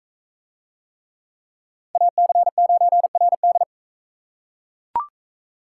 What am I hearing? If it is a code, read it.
Morse code, heard as a keyed tone, about 43 wpm, 700 Hz. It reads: AC9RD